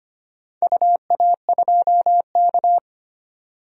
Morse code, heard as a keyed tone, about 25 wpm, 700 Hz. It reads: UA2K
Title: DXCC-100 Call Signs - 4 Characters - 25wpm